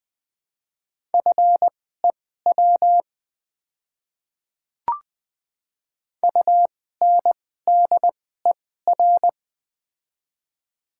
Morse code, heard as a keyed tone, about 20 wpm, spaced as for 14 wpm, 700 Hz. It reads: FEW UNDER
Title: 2x Character Spacing - Top 500 Words - 20wpm